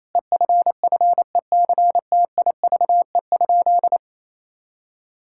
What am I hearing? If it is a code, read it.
Morse code, heard as a keyed tone, about 28 wpm, 700 Hz. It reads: EFFECTIVE?
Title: Sentences from Top 1000 Words - Review Part 6 - 28wpm